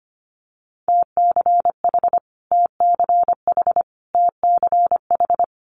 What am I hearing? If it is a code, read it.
Morse code, heard as a keyed tone, about 25 wpm, 700 Hz. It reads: TCH TCH TCH
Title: Multi-letter Phonemes Repeated 3x 25wpm